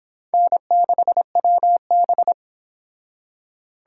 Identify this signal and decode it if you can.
Morse code, heard as a keyed tone, about 25 wpm, 700 Hz. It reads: N6WB